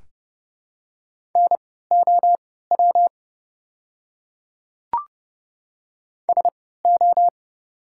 Morse code, heard as a keyed tone, about 30 wpm, spaced as for 17 wpm, 690 Hz. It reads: NOW SO
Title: 3x Character Spacing - Top 100 Words - 30wpm